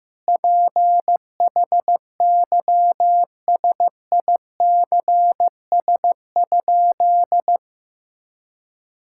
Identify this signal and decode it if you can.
Morse code, heard as a keyed tone, about 15 wpm, 700 Hz. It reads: PHYSICS?
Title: Sentences from Top 1000 Words - Review Part 5 - 15wpm